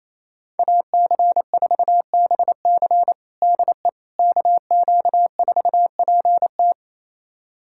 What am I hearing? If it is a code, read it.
Morse code, heard as a keyed tone, about 28 wpm, 700 Hz. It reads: AC4BC DE KQ4PT
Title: USA Callsign DE Callsign - 5 Characters - 28wpm